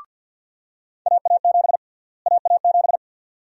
Morse code, heard as a keyed tone, about 51 wpm, 700 Hz. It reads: AA7 AA7
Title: USA Call Sign Prefixes - Repeated 2x - 50wpm